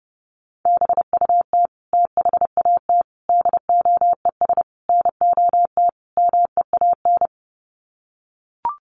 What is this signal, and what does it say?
Morse code, heard as a keyed tone, about 30 wpm, 700 Hz. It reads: BUT THAT DOES NOT MEAN
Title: Sets of 5 Words 30wpm